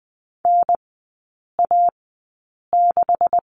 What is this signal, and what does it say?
Morse code, heard as a keyed tone, about 20 wpm, spaced as for 8 wpm, 690 Hz. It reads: NA6
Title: 159 Lesson 26e - Letter g - Three Characters 20wpm